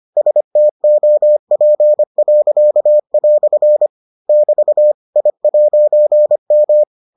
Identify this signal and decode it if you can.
Morse code, heard as a keyed tone, about 25 wpm, 600 Hz. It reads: STOP." <BT> I'M